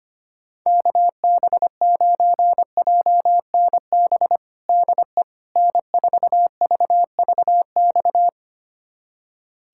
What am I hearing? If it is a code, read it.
Morse code, heard as a keyed tone, about 25 wpm, 700 Hz. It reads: KB9JNB DE N4VVX
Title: USA Callsign DE Callsign 25wpm